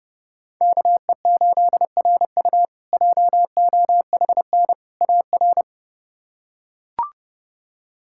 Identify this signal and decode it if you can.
Morse code, heard as a keyed tone, about 30 wpm, 700 Hz. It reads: KE8RU JOHN AR
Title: K1USN SST QSO Practice - 30wpm